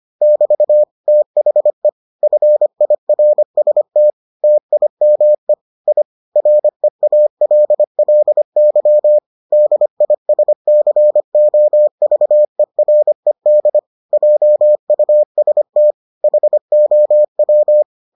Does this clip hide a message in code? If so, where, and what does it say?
Yes: Morse code from 0.2 to 17.8 s, heard as a keyed tone, about 25 wpm, 600 Hz. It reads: <BT> THE FIRST TIME I REALLY DISCOVERED JUST HOW